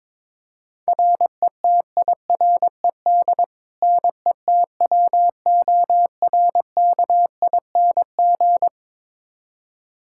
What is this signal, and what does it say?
Morse code, heard as a keyed tone, about 22 wpm, 700 Hz. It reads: RETIRED NETWORKING
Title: K1USN SST QSO Practice 22wpm